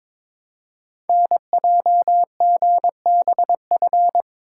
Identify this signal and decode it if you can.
Morse code, heard as a keyed tone, about 23 wpm, 700 Hz. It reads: NJGBF
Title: Code Groups - 5 Letters - 22wpm